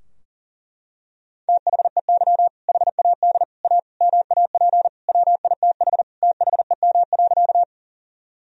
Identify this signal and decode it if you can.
Morse code, heard as a keyed tone, about 37 wpm, 700 Hz. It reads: THEY HAD A MAP WITH THEM.
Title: Sentences from Top 1000 Words - Review Part 2 - 40wpm